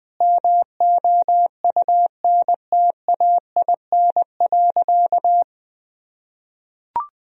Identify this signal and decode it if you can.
Morse code, heard as a keyed tone, about 20 wpm, 700 Hz. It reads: MOUNTAIN.
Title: Sentences from Top 1000 Words - Review Part 1 - 20wpm